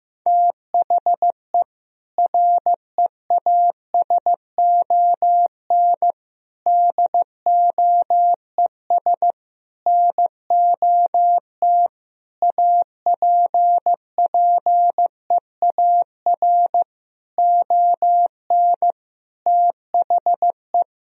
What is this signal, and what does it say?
Morse code, heard as a keyed tone, about 15 wpm, 700 Hz. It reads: THE REASON DOES NOT APPEAR ON THE
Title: Sentences from Top 1000 Words - Review Part 3 - 15wpm